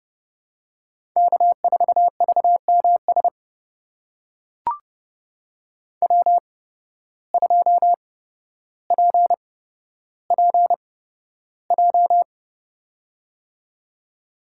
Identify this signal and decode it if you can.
Morse code, heard as a keyed tone, about 30 wpm, spaced as for 13 wpm, 700 Hz. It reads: K4VMS W2PPJ
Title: USA Call Signs - 5 Characters - 8x Character Spacing - 30wpm